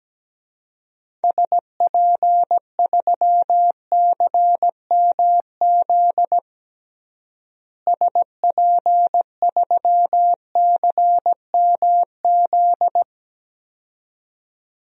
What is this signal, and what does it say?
Morse code, heard as a keyed tone, about 17 wpm, 700 Hz. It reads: SP3CMZ SP3CMZ
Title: DXCC-100 Call Signs - 6 Characters - Repeated 2x - 17wpm